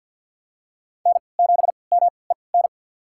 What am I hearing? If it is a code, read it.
Morse code, heard as a keyed tone, about 53 wpm, spaced as for 29 wpm, 690 Hz. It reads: N7MEN
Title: USA Call Signs - 5 Characters - 3x Character Spacing - 50wpm